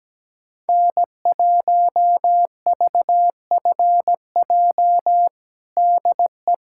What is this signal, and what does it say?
Morse code, heard as a keyed tone, about 17 wpm, 700 Hz. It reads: N1VFJ DE
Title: USA Callsign DE Callsign - 5 Characters - 17wpm